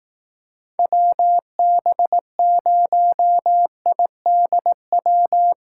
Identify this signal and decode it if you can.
Morse code, heard as a keyed tone, about 18 wpm, 700 Hz. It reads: WB0IDW
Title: Speed Racing - USA 2x3 Call Signs 15wpm